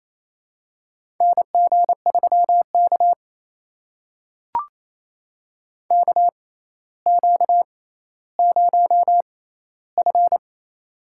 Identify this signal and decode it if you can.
Morse code, heard as a keyed tone, about 28 wpm, spaced as for 14 wpm, 700 Hz. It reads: NG3K KQ0F
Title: USA Call Signs - 4 Characters - 6x Character Spacing - 28wpm